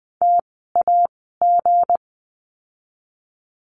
Morse code, heard as a keyed tone, about 20 wpm, spaced as for 15 wpm, 700 Hz. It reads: TAG
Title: 160 Lesson 26f - Letter g - Words 20wpm